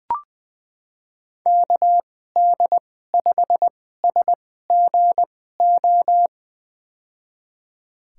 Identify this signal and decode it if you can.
Morse code, heard as a keyed tone, about 20 wpm, spaced as for 14 wpm, 700 Hz. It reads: KD5SGO
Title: USA Call Signs - 6 Characters - 2x Character Spacing - 20wpm